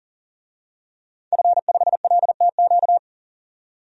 Morse code, heard as a keyed tone, about 40 wpm, 700 Hz. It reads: F5LTQ
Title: DXCC-100 Call Signs - 5 Characters - 40wpm